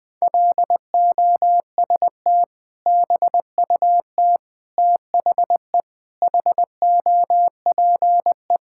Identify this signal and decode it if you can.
Morse code, heard as a keyed tone, about 20 wpm, 700 Hz. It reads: LOST BUT THE HOPE
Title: Sentences with 2-Word Phrases - Spoken Beginning - 20wpm